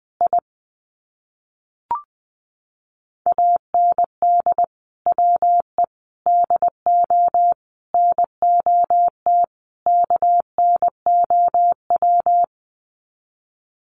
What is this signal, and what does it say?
Morse code, heard as a keyed tone, about 20 wpm, 700 Hz. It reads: I AND WE DO NOT KNOW
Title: Sets of 5 Words Encore 20wpm